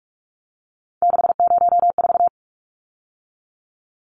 Morse code, heard as a keyed tone, about 44 wpm, 700 Hz. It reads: 604
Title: Numbers - 3-Digit 45wpm